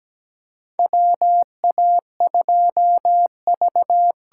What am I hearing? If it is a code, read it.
Morse code, heard as a keyed tone, about 17 wpm, 700 Hz. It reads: WA2V
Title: USA Call Signs - 8x Character Spacing - 17wpm